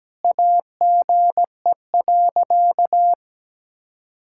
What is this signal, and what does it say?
Morse code, heard as a keyed tone, about 17 wpm, 700 Hz. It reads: AGE.
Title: Sentences from Top 1600 Words 17wpm